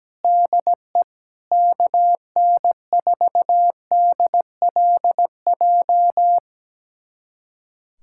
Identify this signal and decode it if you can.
Morse code, heard as a keyed tone, about 17 wpm, 700 Hz. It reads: DE KN4DLJ
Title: USA Callsign DE Callsign - 6 Characters - 17wpm